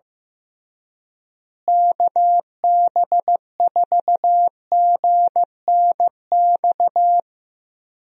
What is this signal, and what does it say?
Morse code, heard as a keyed tone, about 15 wpm, 700 Hz. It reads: KB4GNX
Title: USA Call Signs - 6 Characters - Repeated 2x - 15wpm